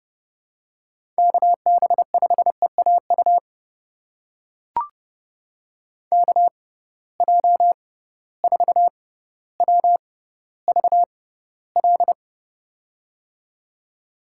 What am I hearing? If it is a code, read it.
Morse code, heard as a keyed tone, about 32 wpm, spaced as for 16 wpm, 700 Hz. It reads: KB5EAU KJ4WVL
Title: USA Call Signs - 6 Characters - 6x Character Spacing - 30wpm